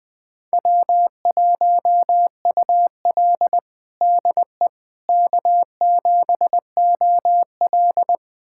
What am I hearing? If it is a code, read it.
Morse code, heard as a keyed tone, about 20 wpm, 700 Hz. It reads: W1UL DE K7OL
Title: USA Callsign DE Callsign - 4 Characters - 20wpm